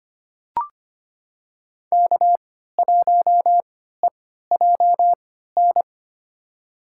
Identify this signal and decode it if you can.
Morse code, heard as a keyed tone, about 25 wpm, spaced as for 14 wpm, 700 Hz. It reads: K1EJN
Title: USA Call Signs - 5 Characters - 3x Character Spacing - 25wpm